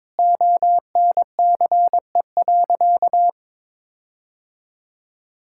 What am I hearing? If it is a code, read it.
Morse code, heard as a keyed tone, about 22 wpm, 700 Hz. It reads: ONCE.
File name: Difficult Sentences with Hints - 22wpm